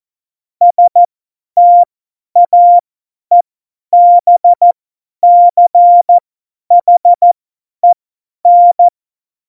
Morse code, heard as a keyed tone, about 14 wpm, spaced as for 10 wpm, 700 Hz. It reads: STAEBCHEN
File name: Morsekurs / Witze / Witz 07 April 2020, 50 F70